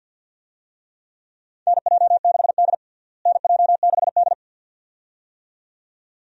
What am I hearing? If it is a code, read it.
Morse code, heard as a keyed tone, about 49 wpm, 700 Hz. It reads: NJ6D NJ6D
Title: USA Call Signs - 4 Characters - Repeated 2x - 50wpm